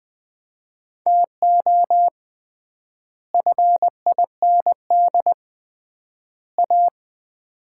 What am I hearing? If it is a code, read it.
Morse code, heard as a keyed tone, about 21 wpm, 700 Hz. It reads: TO FIND A
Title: 3x Word-Spacing - Sets of 3 Words - 20wpm